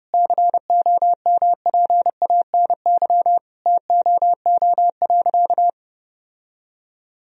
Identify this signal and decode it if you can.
Morse code, heard as a keyed tone, about 30 wpm, 700 Hz. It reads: COMPANY TOO.
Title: Sentences from Top 1000 Words - Review Part 3 - 30wpm